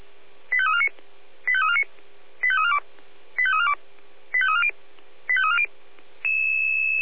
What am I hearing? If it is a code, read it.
Five-tone selective call (ZVEI), heard as quick runs of five tones, tones of about 70 ms. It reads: 85349 85349 85342 85342 85340 85340